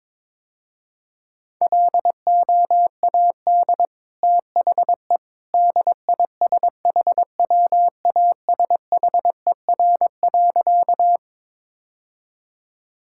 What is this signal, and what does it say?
Morse code, heard as a keyed tone, about 22 wpm, 700 Hz. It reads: LOAD THE DISHWASHER.
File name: practice / 22wpm / Easy Sentences with Hints - 22wpm